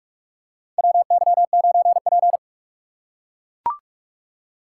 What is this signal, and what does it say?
Morse code, heard as a keyed tone, about 44 wpm, 690 Hz. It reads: WY9P
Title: USA Call Signs - 4 Characters - 3x Character Spacing - 45wpm